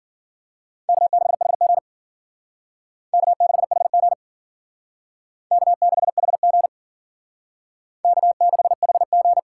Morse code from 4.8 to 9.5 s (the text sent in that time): K6HG K6HG